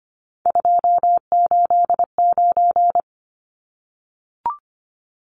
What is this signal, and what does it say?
Morse code, heard as a keyed tone, about 25 wpm, 700 Hz. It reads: 289
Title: Numbers - 3-Digit 25wpm